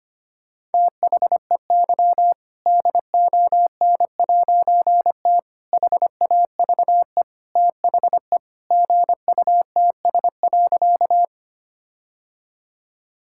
Morse code, heard as a keyed tone, about 25 wpm, 700 Hz. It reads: THEY DON'T HAVE THE GUTS.
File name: practice / 25wpm / All-Age Appropriate Jokes - 25wpm